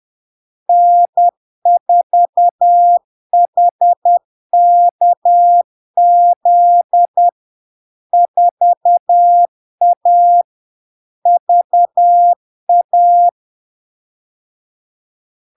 Morse code, heard as a keyed tone, about 10 wpm, 700 Hz. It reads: N4HKZ 4A VA
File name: LARK-FD-2022-Log-in-CW-10wpm-700-Hz